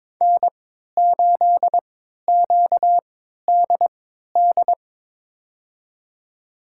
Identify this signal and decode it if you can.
Morse code, heard as a keyed tone, about 21 wpm, spaced as for 12 wpm, 700 Hz. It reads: N8QDD
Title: USA Call Signs - 5 Characters - 3x Character Spacing - 22wpm